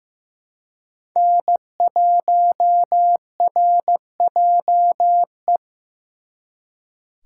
Morse code, heard as a keyed tone, about 18 wpm, 700 Hz. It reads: N1RJE
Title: Speed Racing - USA 1x3 Call Signs 15wpm